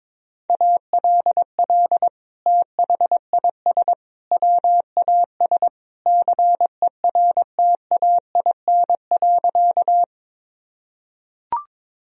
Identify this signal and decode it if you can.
Morse code, heard as a keyed tone, about 22 wpm, 700 Hz. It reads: ALL THIS WAS CERTAIN.
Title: Sentences from Top 1000 Words - Review Part 2 - 22wpm